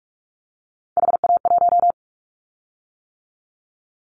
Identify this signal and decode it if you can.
Morse code, heard as a keyed tone, about 46 wpm, 700 Hz. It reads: HA1
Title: Callsign Prefixes for Top 50 DXCC - 45wpm